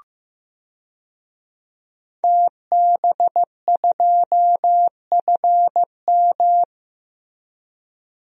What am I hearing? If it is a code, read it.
Morse code, heard as a keyed tone, about 16 wpm, 700 Hz. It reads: TB2FM
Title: DXCC-100 Call Signs - 5 Characters - Repeated 2x - 15wpm